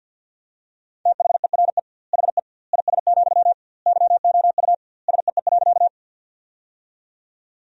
Morse code, heard as a keyed tone, about 50 wpm, 700 Hz. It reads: THERE HE IS, YOU SEE.
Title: 2x Word-Spacing - Sentences from Top 100 Words - 50wpm